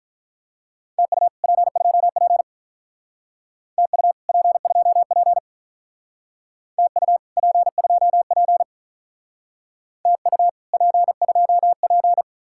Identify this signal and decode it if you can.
Morse code, heard as a keyed tone, about 46 wpm, 700 Hz. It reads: TU P2P TU P2P TU P2P TU P2P